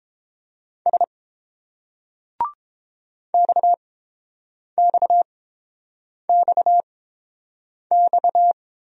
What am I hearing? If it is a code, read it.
Morse code, heard as a keyed tone, about 29 wpm, spaced as for 7 wpm, 700 Hz. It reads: S XXXX